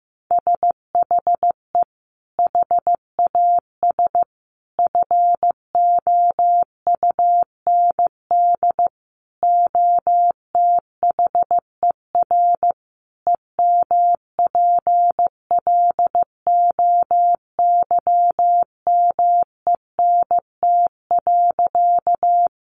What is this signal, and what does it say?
Morse code, heard as a keyed tone, about 15 wpm, 700 Hz. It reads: SHE HAS FOUND OTHER EMPLOYMENT.
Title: Sentences from Top 1500 Words 15wpm